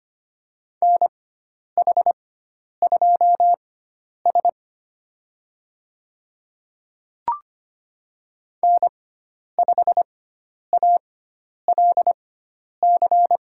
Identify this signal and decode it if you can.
Morse code, heard as a keyed tone, about 25 wpm, spaced as for 10 wpm, 700 Hz. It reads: NH2S N5ALC
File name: Lesson 16g - Letter c - Call Signs 25wpm